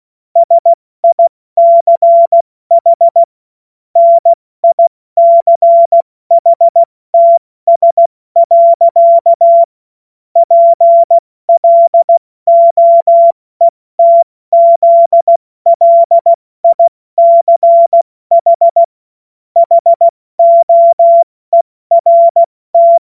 Morse code, heard as a keyed tone, about 16 wpm, spaced as for 14 wpm, 680 Hz. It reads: SICH NICHTS. PLOETZLICH HOERT
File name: Morsekurs / Witze / Witz 09 April 2024 70 F80